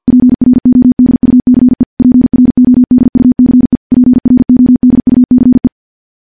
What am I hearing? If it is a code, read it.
Morse code, heard as a keyed tone, about 50 wpm, 250 Hz. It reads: PRODUCE PRODUCE PRODUCE
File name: morse / morse-training-50WPM